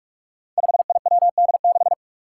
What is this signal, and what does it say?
Morse code, heard as a keyed tone, about 44 wpm, 700 Hz. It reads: 5IWD6
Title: Five Challenging Letter-Number Combinations 45wpm